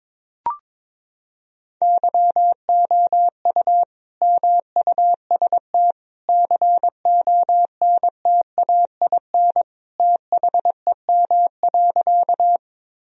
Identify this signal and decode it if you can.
Morse code, heard as a keyed tone, about 22 wpm, 700 Hz. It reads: YOU MUST CONTAIN THEM.
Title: Sentences from Top 1000 Words - Review Part 2 - 22wpm